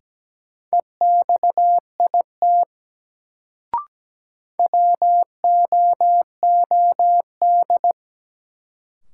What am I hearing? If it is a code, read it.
Morse code, heard as a keyed tone, about 17 wpm, 700 Hz. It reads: EXIT WOOD
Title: Sentences from Top 1100 Words - New Word Drill Alt - 17wpm